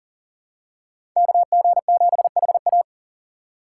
Morse code, heard as a keyed tone, about 37 wpm, 700 Hz. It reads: KG7HA
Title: USA Call Signs - 4x Character Spacing - 40wpm